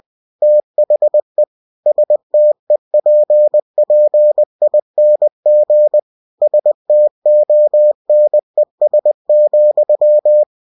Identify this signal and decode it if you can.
Morse code, heard as a keyed tone, about 20 wpm, 600 Hz. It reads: THE STEPPING STONES,
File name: cw-books / winnie-the-pooh / winnie-20wpm-page-0034